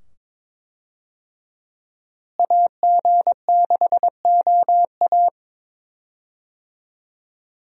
Morse code, heard as a keyed tone, about 22 wpm, 700 Hz. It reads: AG6OA
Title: USA Call Signs - 5 Characters - Repeated 2x - 22wpm